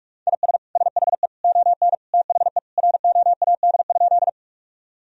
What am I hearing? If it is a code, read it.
Morse code, heard as a keyed tone, about 46 wpm, 700 Hz. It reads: IS SHE ON THE ROAD?
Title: Sentences from Top 1000 Words - Review Part 2 - 45wpm